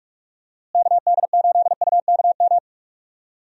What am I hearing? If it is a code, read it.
Morse code, heard as a keyed tone, about 42 wpm, 700 Hz. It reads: KD8UKM
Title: USA Call Signs - 6 Characters - Repeated 2x - 45wpm